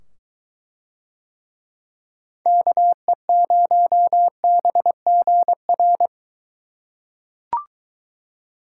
Morse code, heard as a keyed tone, about 23 wpm, 700 Hz. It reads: KE0BGR